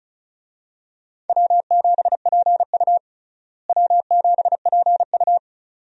Morse code, heard as a keyed tone, about 36 wpm, 700 Hz. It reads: W7PU W7PU
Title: USA Call Signs - 4 Characters - Repeated 2x - 35wpm